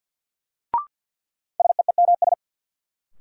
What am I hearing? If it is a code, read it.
Morse code, heard as a keyed tone, about 50 wpm, 700 Hz. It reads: SEEMS
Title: Sets of 3 Words - New Word Drill - 50wpm